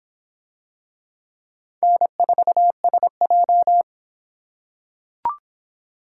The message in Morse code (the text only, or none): N4SJ